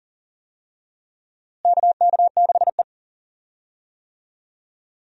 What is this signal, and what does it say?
Morse code, heard as a keyed tone, about 40 wpm, 700 Hz. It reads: KK6E
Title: USA Call Signs - 4 Characters - Repeated 3x - 40wpm